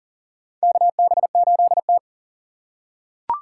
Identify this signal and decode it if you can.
Morse code, heard as a keyed tone, about 37 wpm, 700 Hz. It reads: KB8T
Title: USA Call Signs - 4 Characters - 2x Character Spacing - 40wpm